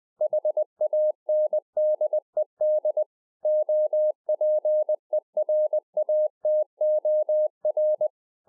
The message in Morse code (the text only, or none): HANDED OPERATOR